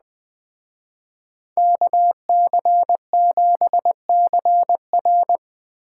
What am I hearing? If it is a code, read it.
Morse code, heard as a keyed tone, about 20 wpm, 700 Hz. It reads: KC7CR